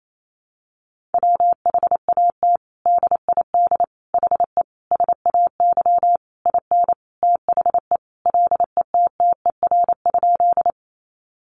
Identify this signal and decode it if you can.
Morse code, heard as a keyed tone, about 28 wpm, 700 Hz. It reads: WHAT DID HE SAY IN THE LETTER?
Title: Sentences from Top 300 Words 28wpm